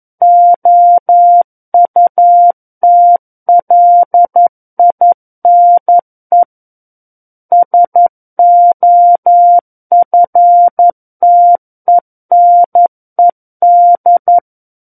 Morse code, heard as a keyed tone, about 11 wpm, 700 Hz. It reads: OUTLINE SOFTENED
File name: 11 WPM